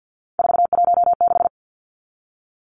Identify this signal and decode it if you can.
Morse code, heard as a keyed tone, about 48 wpm, 700 Hz. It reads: 416
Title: Numbers - 3-Digit 50wpm